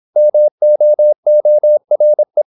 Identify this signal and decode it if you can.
Morse code, heard as a keyed tone, about 26 wpm, 600 Hz. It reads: MOORE
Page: Never 26wpm | CW med Gnister